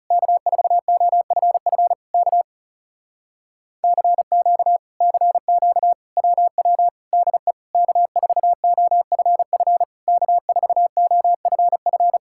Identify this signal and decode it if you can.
Morse code, heard as a keyed tone, about 40 wpm, 700 Hz. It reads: K4OFF K CQ CQ WW DE K4OFF K4OFF